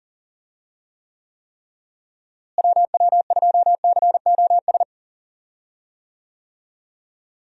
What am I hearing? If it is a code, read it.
Morse code, heard as a keyed tone, about 40 wpm, 700 Hz. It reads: WW2COS
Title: USA Call Signs - 6 Characters - Repeated 3x - 40wpm